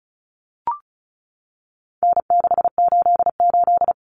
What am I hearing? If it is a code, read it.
Morse code, heard as a keyed tone, about 34 wpm, 700 Hz. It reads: N688
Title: Common Cut Numbers - 4-Digit 35wpm